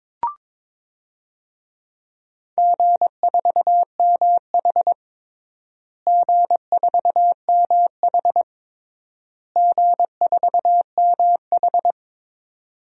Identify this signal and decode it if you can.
Morse code, heard as a keyed tone, about 22 wpm, 700 Hz. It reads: G4MH G4MH G4MH
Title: DXCC-100 Call Signs - 4 Characters - Repeated 3x - 22wpm